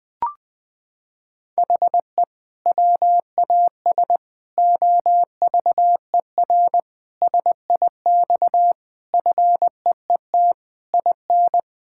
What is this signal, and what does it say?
Morse code, heard as a keyed tone, about 21 wpm, 700 Hz. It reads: HE WAS OVER SIX FEET IN
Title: Sentences from Top 1600 Words 20wpm